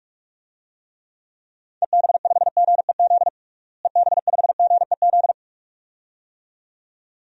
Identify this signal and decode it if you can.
Morse code, heard as a keyed tone, about 47 wpm, 700 Hz. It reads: EB5GEZ EB5GEZ
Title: DXCC-100 Call Signs - 6 Characters - Repeated 2x - 45wpm